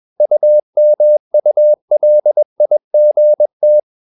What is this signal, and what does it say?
Morse code, heard as a keyed tone, about 21 wpm, 600 Hz. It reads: UMULIGT
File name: Ferie QRS 21wpm